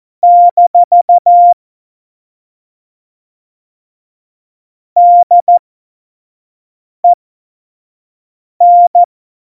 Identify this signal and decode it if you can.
Morse code, heard as a keyed tone, about 14 wpm, spaced as for 5 wpm, 700 Hz. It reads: - DEN
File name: Morsekurs / Witze / Witz 15. September 2020, 25 F70